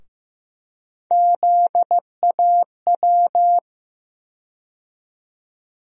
Morse code, heard as a keyed tone, about 16 wpm, 700 Hz. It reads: ZAW
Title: USA Call Sign Postfixes - 3 Letters 15wpm